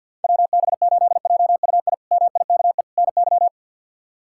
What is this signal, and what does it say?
Morse code, heard as a keyed tone, about 53 wpm, 690 Hz. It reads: WB8JUI MIKE NY
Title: K1USN SST QSO Practice - 50wpm